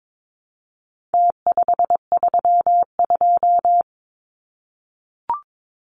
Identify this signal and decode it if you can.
Morse code, heard as a keyed tone, about 22 wpm, 700 Hz. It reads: T532
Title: Numbers - Contest Style - 4-Digit 22wpm